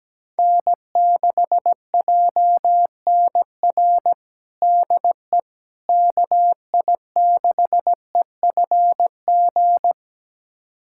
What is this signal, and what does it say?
Morse code, heard as a keyed tone, about 17 wpm, 700 Hz. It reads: N6JNR DE KI6EFG
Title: USA Callsign DE Callsign 17wpm